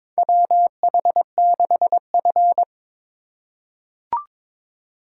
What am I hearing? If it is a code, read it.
Morse code, heard as a keyed tone, about 21 wpm, 700 Hz. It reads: WH6F
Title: USA Call Signs - 4 Characters - 2x Character Spacing - 22wpm